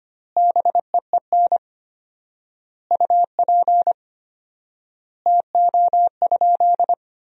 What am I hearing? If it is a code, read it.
Morse code, heard as a keyed tone, about 25 wpm, 700 Hz. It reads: BEEN UP TO?
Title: 4x Word-Spacing - Sentences from Top 100 Words - 25wpm